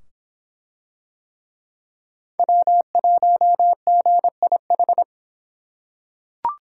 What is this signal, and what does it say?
Morse code, heard as a keyed tone, about 24 wpm, 700 Hz. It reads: W1GIH